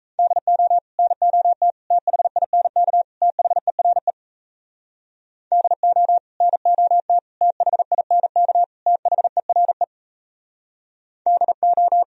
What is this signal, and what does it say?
Morse code, heard as a keyed tone, about 38 wpm, 690 Hz. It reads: DO NOT THINK THERE DO NOT THINK THERE DO